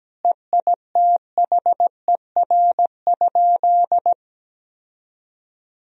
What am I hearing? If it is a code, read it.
Morse code, heard as a keyed tone, about 17 wpm, 700 Hz. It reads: EITHER?
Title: Sentences from Top 1000 Words - Review Part 4 - 17wpm